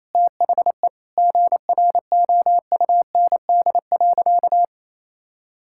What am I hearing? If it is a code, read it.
Morse code, heard as a keyed tone, about 28 wpm, 700 Hz. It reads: THE GROUND.
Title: Sentences from Top 1600 Words 28wpm